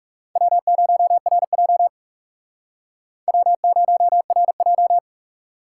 Morse code, heard as a keyed tone, about 40 wpm, 700 Hz. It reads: W0RJ W0RJ